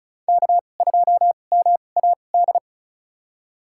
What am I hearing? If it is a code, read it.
Morse code, heard as a keyed tone, about 34 wpm, spaced as for 25 wpm, 700 Hz. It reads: K2MAD
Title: USA Call Signs - 5 Characters - 2x Character Spacing - 35wpm